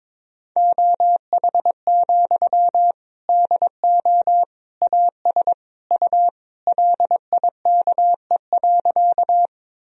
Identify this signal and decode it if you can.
Morse code, heard as a keyed tone, about 22 wpm, 700 Hz. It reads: OH, DO AS U LIKE.
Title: Sentences from Top 500 Words and Abbreviations 22wpm